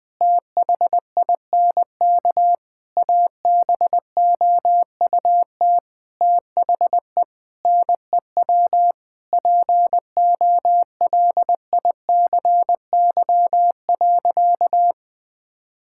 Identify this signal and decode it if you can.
Morse code, heard as a keyed tone, about 21 wpm, 700 Hz. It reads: THINK ABOUT THE NEW POLICY.
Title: Sentences with 8-Word Phrases - 20wpm